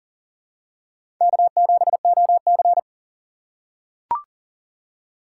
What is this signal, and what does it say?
Morse code, heard as a keyed tone, about 38 wpm, 700 Hz. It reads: K7OC
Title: USA Call Signs - 4 Characters - 3x Character Spacing - 40wpm